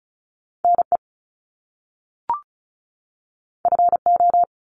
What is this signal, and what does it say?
Morse code, heard as a keyed tone, about 36 wpm, 700 Hz. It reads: NE FO
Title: Top Two Letter Combinations in Words 35wpm